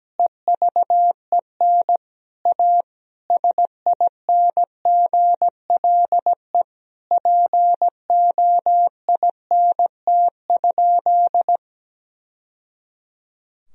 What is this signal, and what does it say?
Morse code, heard as a keyed tone, about 17 wpm, 700 Hz. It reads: EVEN A SINGLE POINT?
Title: Sentences from Top 1000 Words - Review Part 4 - 17wpm